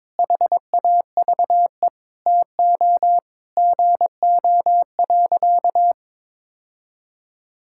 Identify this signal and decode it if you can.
Morse code, heard as a keyed tone, about 22 wpm, 700 Hz. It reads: HAVE TO GO.
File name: practice / 22wpm / Sentences with 3-Word Phrases - 22wpm